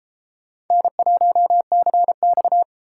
Morse code, heard as a keyed tone, about 33 wpm, 700 Hz. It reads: N1CX